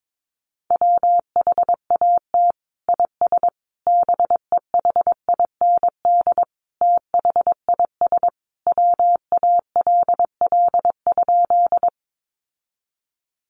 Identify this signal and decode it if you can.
Morse code, heard as a keyed tone, about 22 wpm, 700 Hz. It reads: WHAT IS BEHIND THIS WALL?
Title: Sentences from Top 1100 Words 22wpm